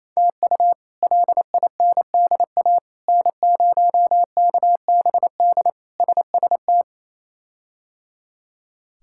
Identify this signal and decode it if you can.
Morse code, heard as a keyed tone, about 28 wpm, 700 Hz. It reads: TU LINDA N0KBD SST
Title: K1USN SST QSO Practice - 28wpm